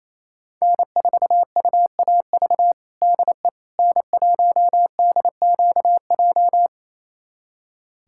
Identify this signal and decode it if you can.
Morse code, heard as a keyed tone, about 28 wpm, 700 Hz. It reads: N4UAV DE N1DQJ
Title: USA Callsign DE Callsign - 5 Characters - 28wpm